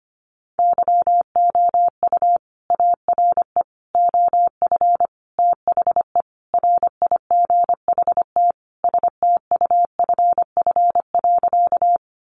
Morse code, heard as a keyed tone, about 25 wpm, 700 Hz. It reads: YOU ARE OF THE RIGHT STUFF.
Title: Sentences from Top 1400 Words 25wpm